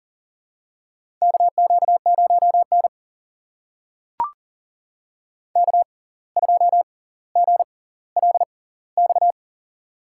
Morse code, heard as a keyed tone, about 40 wpm, spaced as for 19 wpm, 700 Hz. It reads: KQ0N K2GLX
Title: USA Call Signs - 6x Character Spacing - 40wpm